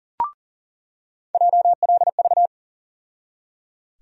Morse code, heard as a keyed tone, about 37 wpm, 700 Hz. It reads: JLV
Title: USA Call Sign Postfixes - 3 Letters 40wpm